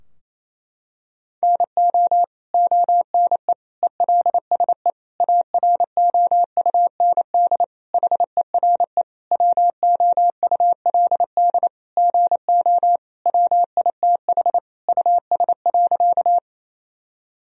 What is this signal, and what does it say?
Morse code, heard as a keyed tone, about 28 wpm, 700 Hz. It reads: NO ONE ELSE AROUND HERE WOULD GO WITH US.
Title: Sentences from Top 1000 Words - Review Part 3 - 28wpm